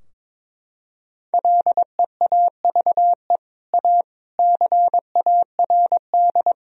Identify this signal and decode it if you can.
Morse code, heard as a keyed tone, about 22 wpm, 700 Hz. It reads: LEAVE A CARD
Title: Sentences from Top 1000 Words - Review Part 4 - 22wpm